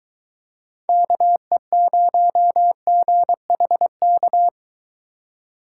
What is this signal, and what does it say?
Morse code, heard as a keyed tone, about 23 wpm, 700 Hz. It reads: KE0GHK